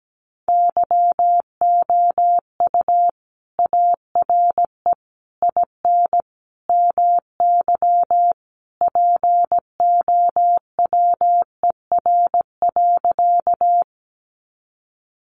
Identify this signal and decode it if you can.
Morse code, heard as a keyed tone, about 17 wpm, 700 Hz. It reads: YOU ARE IN MY POWER.
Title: Sentences from Top 500 Words 17wpm